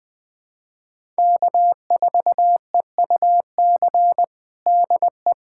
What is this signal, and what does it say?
Morse code, heard as a keyed tone, about 21 wpm, 700 Hz. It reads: K4EUC DE
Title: USA Callsign DE Callsign - 5 Characters - 20wpm